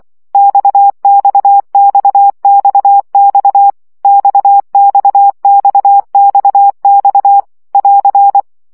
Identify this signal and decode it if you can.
Morse code, heard as a keyed tone, about 24 wpm, 800 Hz. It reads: XXXXX XXXXX <AR>
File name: CW gyakorlas x 20 baud